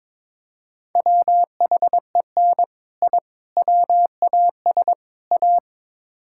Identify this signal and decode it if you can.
Morse code, heard as a keyed tone, about 22 wpm, 700 Hz. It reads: WHEN I WAS A
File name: practice / 22wpm / Build A Phrase - 5-Word Phrases - 22wpm